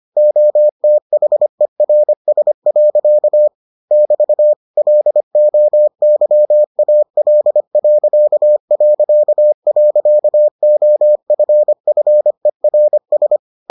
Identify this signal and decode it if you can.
Morse code, heard as a keyed tone, about 25 wpm, 600 Hz. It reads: OTHERS. <BT> LOYAL...OFFERS